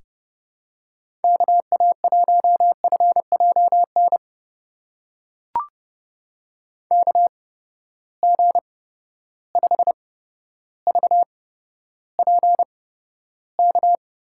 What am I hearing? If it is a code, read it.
Morse code, heard as a keyed tone, about 30 wpm, spaced as for 13 wpm, 700 Hz. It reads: KA1FJN KG5VPK
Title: USA Call Signs - 6 Characters - 8x Character Spacing - 30wpm